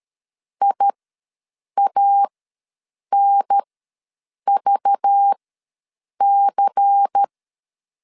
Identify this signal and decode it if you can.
Morse code, heard as a keyed tone, about 13 wpm, spaced as for 7 wpm, 800 Hz. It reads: IANVC
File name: LEZIONI / 7 GF9Z / 7 GF9 ws 12wpm 3344